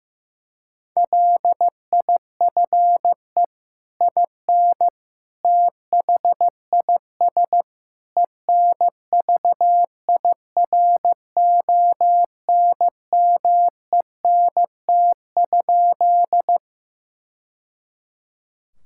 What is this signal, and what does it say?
Morse code, heard as a keyed tone, about 15 wpm, 700 Hz. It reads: LIFE IN THIS ENVIRONMENT?
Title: Sentences from Top 1000 Words - Review Part 5 - 15wpm